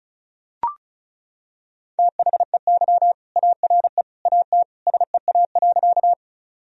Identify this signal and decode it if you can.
Morse code, heard as a keyed tone, about 35 wpm, 700 Hz. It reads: THEY ARE AT SEA.
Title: Sentences from Top 1000 Words - Review Part 1 - 35wpm